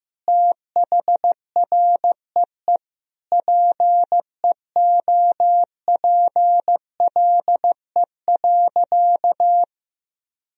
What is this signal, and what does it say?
Morse code, heard as a keyed tone, about 15 wpm, 700 Hz. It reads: THREE PEOPLE.
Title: Sentences from Top 1000 Words - Review Part 1 - 15wpm